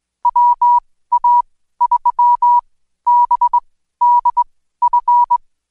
Morse code, heard as a keyed tone, about 21 wpm, spaced as for 14 wpm, 970 Hz. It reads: WA3BDF
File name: Code Quick / AudioDownld2 / Seg124-Day29-DayDrill 29